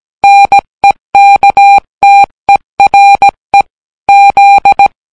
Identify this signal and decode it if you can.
Morse code, heard as a keyed tone, about 17 wpm, 800 Hz. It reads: NEKTERE Z